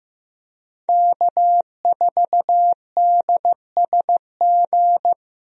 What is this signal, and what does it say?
Morse code, heard as a keyed tone, about 16 wpm, 700 Hz. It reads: K4DSG
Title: USA Call Signs - 5 Characters - Repeated 2x - 15wpm